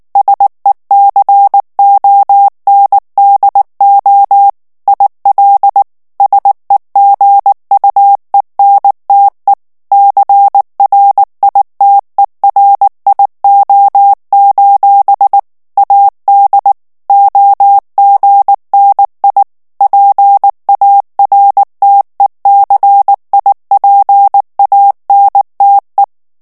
Morse code, heard as a keyed tone, about 19 wpm, 800 Hz. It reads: SECONDO IL SEGUENTE CRITERIO: AD OGNI PARTECIPANTE
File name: morse mp3 / ITALIEN texte clair 2 vitesse 18 mots minute